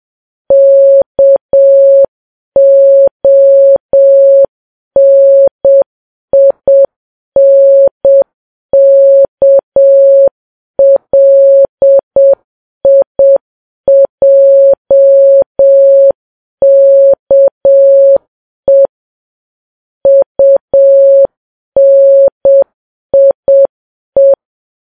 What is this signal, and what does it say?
Morse code, heard as a keyed tone, about 7 wpm, 550 Hz. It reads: KONINKLIJKE UNIE
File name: Files Web tls / Morse on9fv / morse7kps